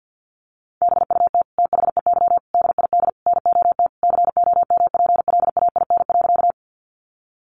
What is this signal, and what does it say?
Morse code, heard as a keyed tone, about 50 wpm, 700 Hz. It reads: BUT THEY DID NOT COMPLAIN.
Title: Sentences from Top 1300 Words 50wpm